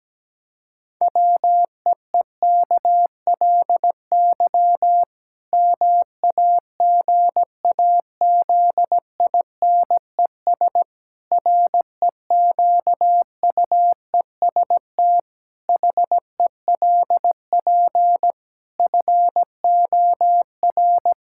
Morse code, heard as a keyed tone, about 17 wpm, 700 Hz. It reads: WEEKLY MAGAZINES REQUEST HELP FOR
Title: Pangram Challenge v2 - 17wpm